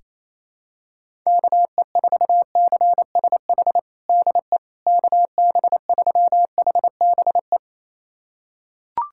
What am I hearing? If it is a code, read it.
Morse code, heard as a keyed tone, about 28 wpm, 700 Hz. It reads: KE4CSH DE KB3HBE
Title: USA Callsign DE Callsign 28wpm